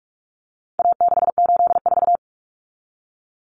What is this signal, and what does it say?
Morse code, heard as a keyed tone, about 44 wpm, 700 Hz. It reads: A684
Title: Common Cut Numbers - 4-Digit 45wpm